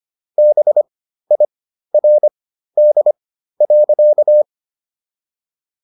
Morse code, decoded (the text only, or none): BIRD.